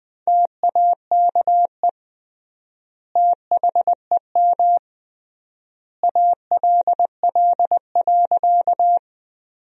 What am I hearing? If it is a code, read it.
Morse code, heard as a keyed tone, about 21 wpm, 700 Hz. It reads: TAKE THEM ALL.
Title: 3x Word-Spacing - Sentences from Top 200 Words - 20wpm